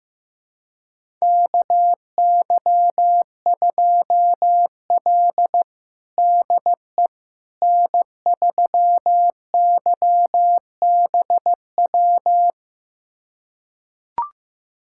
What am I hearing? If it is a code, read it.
Morse code, heard as a keyed tone, about 15 wpm, 700 Hz. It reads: KY2L DE N3YBW
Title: USA Callsign DE Callsign 15wpm